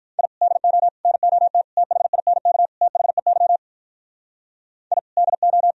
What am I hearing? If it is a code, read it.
Morse code, heard as a keyed tone, about 44 wpm, 700 Hz. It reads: I DO NOT THINK THEY I DO